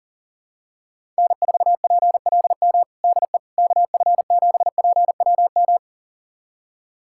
Morse code, heard as a keyed tone, about 40 wpm, 700 Hz. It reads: N4PLM DE KF7PWM
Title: USA Callsign DE Callsign 40wpm